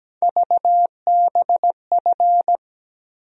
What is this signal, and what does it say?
Morse code, heard as a keyed tone, about 17 wpm, 700 Hz. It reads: VBF
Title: USA Call Sign Postfixes - 3 Letters 17wpm